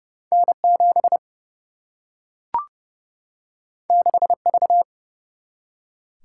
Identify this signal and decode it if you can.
Morse code, heard as a keyed tone, about 30 wpm, 700 Hz. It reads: N 7 6 V
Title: Single-Letter Single-Number 30wpm